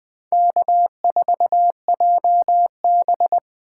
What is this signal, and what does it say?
Morse code, heard as a keyed tone, about 20 wpm, 700 Hz. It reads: K4JB